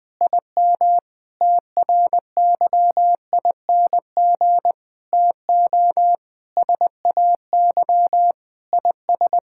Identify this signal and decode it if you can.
Morse code, heard as a keyed tone, about 20 wpm, 700 Hz. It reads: IM TRYING TO SAY IS
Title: Sentences with 8-Word Phrases - 20wpm